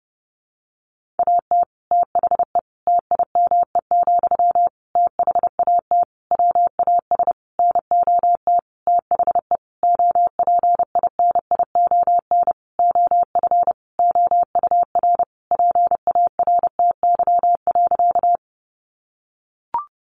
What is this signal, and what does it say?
Morse code, heard as a keyed tone, about 30 wpm, 700 Hz. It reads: AT THE TIME, THAT WAS NOT THE OPINION OF OUR PARTY.
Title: Sentences from Top 1400 Words 30wpm